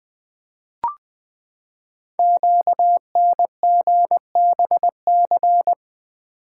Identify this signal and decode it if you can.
Morse code, heard as a keyed tone, about 20 wpm, 700 Hz. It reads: QNGBC
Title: Code Groups - 5 Letters - 20wpm